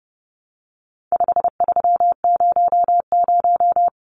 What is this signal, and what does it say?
Morse code, heard as a keyed tone, about 29 wpm, 700 Hz. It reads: 5300
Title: Numbers - 4-Digit 30wpm